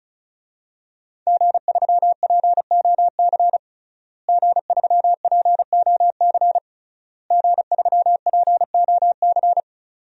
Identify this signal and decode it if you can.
Morse code, heard as a keyed tone, about 36 wpm, 700 Hz. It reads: G3POC G3POC G3POC
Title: DXCC-100 Call Signs - 5 Characters - Repeated 3x - 35wpm